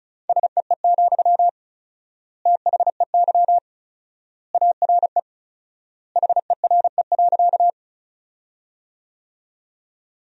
Morse code, heard as a keyed tone, about 34 wpm, 700 Hz. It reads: SEE, THEY ARE HERE.
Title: 4x Word-Spacing - Sentences from Top 200 Words - 35wpm